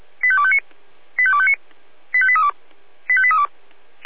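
Five-tone selective call (ZVEI), heard as quick runs of five tones, tones of about 70 ms. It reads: 85259 85259 85832 85832
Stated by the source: ZVEI five-tone call received on SDR.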